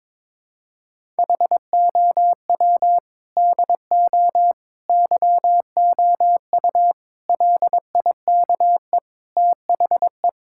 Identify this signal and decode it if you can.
Morse code, heard as a keyed tone, about 22 wpm, 700 Hz. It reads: HOW DO YOU LIKE THE
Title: Sentences from Top 1000 Words - Review Part 6 - 22wpm